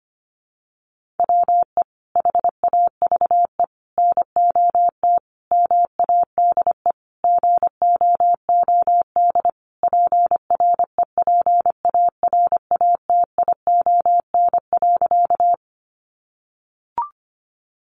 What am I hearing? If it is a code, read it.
Morse code, heard as a keyed tone, about 25 wpm, 700 Hz. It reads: WE HAVE NOT MADE GOOD PREPARATION.
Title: Sentences from Top 1400 Words 25wpm